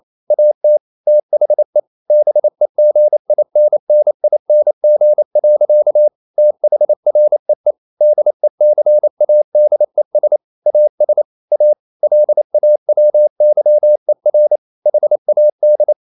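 Morse code, heard as a keyed tone, about 28 wpm, 600 Hz. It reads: AT THE BEGINNING. THREE DECADES AS A LAWYER HAD
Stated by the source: Morse code practice at 28 wpm, text of Ken Follett's novel Never in English, from CW med Gnister.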